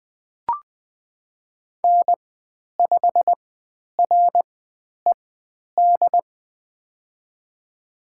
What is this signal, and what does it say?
Morse code, heard as a keyed tone, about 20 wpm, spaced as for 10 wpm, 700 Hz. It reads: N5RED
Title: 102 Lesson 17g - Letter u - Call Signs 20wpm